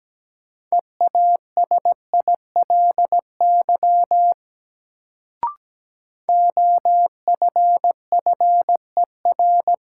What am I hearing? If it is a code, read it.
Morse code, heard as a keyed tone, about 17 wpm, 700 Hz. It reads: EASILY OFFER
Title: Sentences from Top 900 Words - New Word Drill - 17wpm